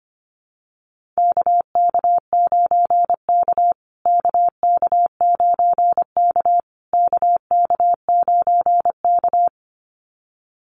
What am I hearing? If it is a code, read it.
Morse code, heard as a keyed tone, about 25 wpm, 700 Hz. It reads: KK9K KK9K KK9K
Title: USA Call Signs Repeated 3x 25wpm